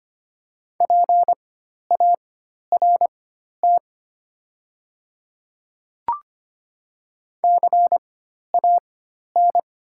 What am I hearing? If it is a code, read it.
Morse code, heard as a keyed tone, about 25 wpm, spaced as for 12 wpm, 690 Hz. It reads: PART CAN
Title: 4x Character Spacing - Top 200 Words - 25wpm